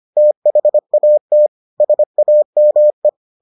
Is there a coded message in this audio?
Yes: Morse code from 0.2 to 3.1 s, heard as a keyed tone, about 25 wpm, 600 Hz. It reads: THAT SAME